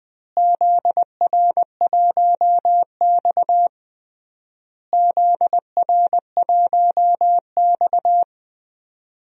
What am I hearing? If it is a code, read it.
Morse code, heard as a keyed tone, about 20 wpm, 700 Hz. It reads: ZR1X ZR1X
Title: DXCC-100 Call Signs - 4 Characters - Repeated 3x - 20wpm